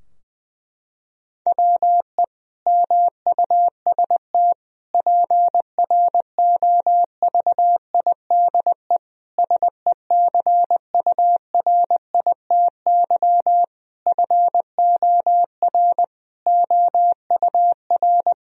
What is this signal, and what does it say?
Morse code, heard as a keyed tone, about 21 wpm, 700 Hz. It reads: WE MUST PROVIDE SECURITY FOR OUR
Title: Sentences from Top 1000 Words - Review Part 4 - 20wpm